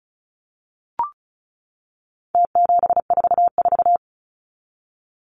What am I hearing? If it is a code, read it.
Morse code, heard as a keyed tone, about 34 wpm, 700 Hz. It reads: T744
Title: Numbers - Contest Style - 4-Digit 35wpm